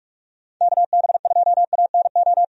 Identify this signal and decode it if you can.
Morse code, heard as a keyed tone, about 43 wpm, 700 Hz. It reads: KB2ANO
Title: USA Call Signs - 6 Characters - Repeated 3x - 45wpm